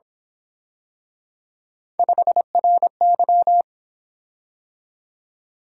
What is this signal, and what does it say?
Morse code, heard as a keyed tone, about 26 wpm, 700 Hz. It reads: 5RY
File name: Speed-Racing - Three Letter-Number Combinations - 22wpm